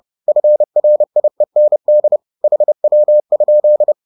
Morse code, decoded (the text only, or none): FRIEND HW?